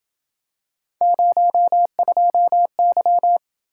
Morse code, heard as a keyed tone, about 26 wpm, 700 Hz. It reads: 02Y